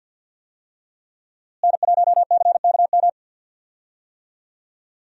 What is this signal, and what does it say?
Morse code, heard as a keyed tone, about 46 wpm, 700 Hz. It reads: N1CKM
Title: USA Call Signs - 5 Characters - Repeated 3x - 50wpm